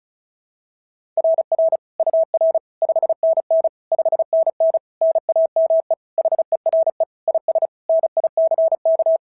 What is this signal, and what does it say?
Morse code, heard as a keyed tone, about 36 wpm, 690 Hz. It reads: RR UR 5NN 5NN NAME HERE IS NICK
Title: SOTA QSO Practice 35wpm